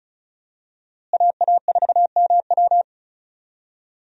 Morse code, heard as a keyed tone, about 36 wpm, 700 Hz. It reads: AA4MW
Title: USA Call Signs - 5 Characters - Repeated 2x - 35wpm